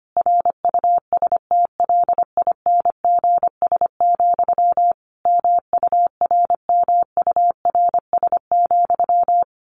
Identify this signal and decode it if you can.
Morse code, heard as a keyed tone, about 25 wpm, 700 Hz. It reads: RUSTLINGS, MURMURS,
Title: A Sound Of Thunder 25wpm